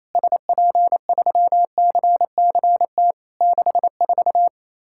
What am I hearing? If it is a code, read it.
Morse code, heard as a keyed tone, about 27 wpm, 700 Hz. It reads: SP3CCT 64
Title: Worked All Europe DX Contest - QTC Exchange - 28wpm